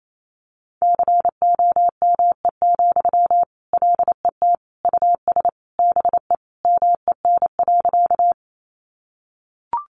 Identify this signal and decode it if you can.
Morse code, heard as a keyed tone, about 28 wpm, 700 Hz. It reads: COME, LET US BE MEN.
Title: Sentences from Top 300 Words 28wpm